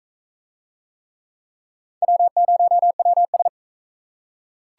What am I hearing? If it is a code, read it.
Morse code, heard as a keyed tone, about 38 wpm, 700 Hz. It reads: W0WS